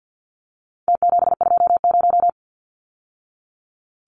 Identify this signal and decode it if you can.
Morse code, heard as a keyed tone, about 49 wpm, 700 Hz. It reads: T720
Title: Numbers - Contest Style - 4-Digit 50wpm